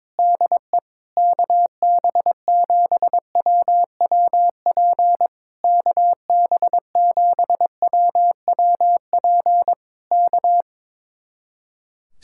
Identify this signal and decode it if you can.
Morse code, heard as a keyed tone, about 22 wpm, 700 Hz. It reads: DE KB7WWP KB7WWP K
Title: Speed-Racing Common QSO Phrases Encore - 22wpm